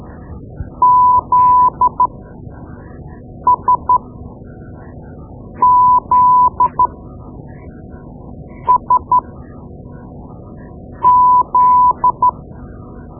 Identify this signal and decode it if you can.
Morse code, heard as a keyed tone, about 13 wpm, spaced as for 4 wpm, 1000 Hz. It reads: ZSZSZ
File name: Morse Lessons / Morse Lessons / Tape-2B